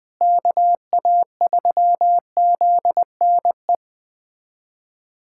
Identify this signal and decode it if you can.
Morse code, heard as a keyed tone, about 21 wpm, 700 Hz. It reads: KA3ZNE
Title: USA Callsign DE Callsign - 6 Characters - 20wpm